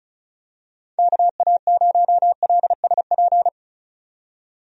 Morse code, heard as a keyed tone, about 36 wpm, 700 Hz. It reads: KA0LSP
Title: USA Call Signs - 6 Characters - 35wpm